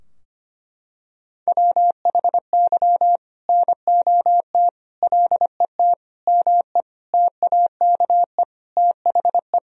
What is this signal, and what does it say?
Morse code, heard as a keyed tone, about 25 wpm, 700 Hz. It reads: WHY NOT LET ME TAKE THE
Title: Sentences from Top 1000 Words - Review Part 2 - 25wpm